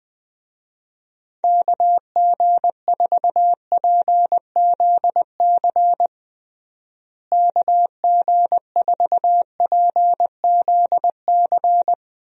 Morse code, heard as a keyed tone, about 20 wpm, 700 Hz. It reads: KG4PZC KG4PZC
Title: USA Call Signs - 6 Characters - Repeated 3x - 20wpm